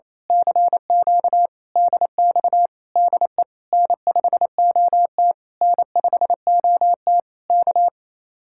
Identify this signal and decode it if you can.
Morse code, heard as a keyed tone, about 26 wpm, 700 Hz. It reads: CQ DX DE N5OT N5OT K
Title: Speed-Racing Common QSO Phrases - 28wpm